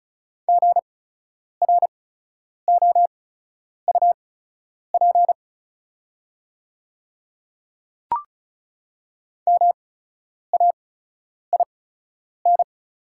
Morse code, heard as a keyed tone, about 36 wpm, spaced as for 10 wpm, 700 Hz. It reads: GROUP MAIN
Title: 8x Character Spacing - Top 300 Words - 35wpm